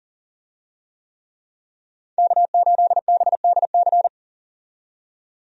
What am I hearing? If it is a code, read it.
Morse code, heard as a keyed tone, about 40 wpm, 700 Hz. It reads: K8BDC
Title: USA Call Signs - 5 Characters - 40wpm